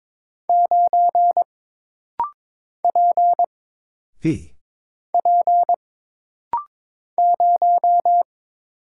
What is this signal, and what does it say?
Morse code, heard as a keyed tone, about 22 wpm, 700 Hz. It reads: 9 P P 0
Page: This rapid-fire timing is geared to push your ICR (Instant Character Recognition) skills.